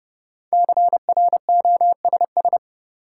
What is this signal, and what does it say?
Morse code, heard as a keyed tone, about 32 wpm, 700 Hz. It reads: CROSS
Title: Sentences from Top 300 Words - New Word Drill - 30wpm